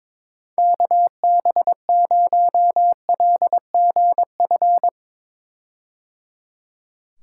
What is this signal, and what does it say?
Morse code, heard as a keyed tone, about 22 wpm, 700 Hz. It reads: KB0LGF
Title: K1USN SST QSO Practice - 22wpm